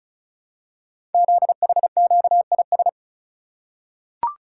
Morse code, heard as a keyed tone, about 36 wpm, 700 Hz. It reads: ZHQIS
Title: Code Groups - 5 Letters - 35wpm